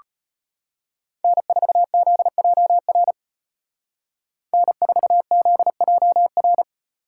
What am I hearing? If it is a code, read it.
Morse code, heard as a keyed tone, about 32 wpm, 700 Hz. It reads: N4ZJR N4ZJR